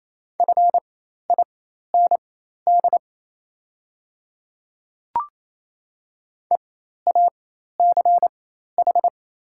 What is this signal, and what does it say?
Morse code, heard as a keyed tone, about 27 wpm, spaced as for 13 wpm, 700 Hz. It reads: FIND EACH
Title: 4x Character Spacing - Top 100 Words - 28wpm